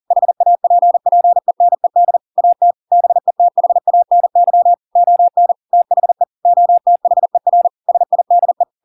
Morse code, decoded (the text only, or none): HAPPENED AT BETHANY ON THE OTHER SIDE